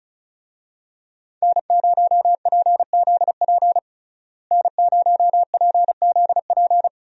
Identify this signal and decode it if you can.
Morse code, heard as a keyed tone, about 36 wpm, 700 Hz. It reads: N0PZP N0PZP
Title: USA Call Signs - 5 Characters - Repeated 3x - 35wpm